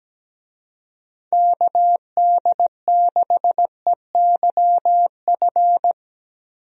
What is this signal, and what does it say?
Morse code, heard as a keyed tone, about 17 wpm, 700 Hz. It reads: KD6EYF
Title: USA Call Signs - 6 Characters - 17wpm